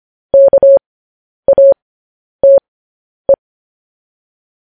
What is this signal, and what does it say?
Morse code, heard as a keyed tone, about 25 wpm, spaced as for 10 wpm, 550 Hz. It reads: KATE
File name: podcast / News headlines-1x-25x10